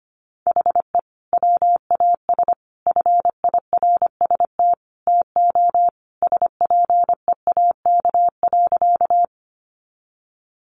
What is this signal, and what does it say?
Morse code, heard as a keyed tone, about 25 wpm, 700 Hz. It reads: HE WAS FIRST TO SPEAK.
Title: Sentences from Top 1100 Words 25wpm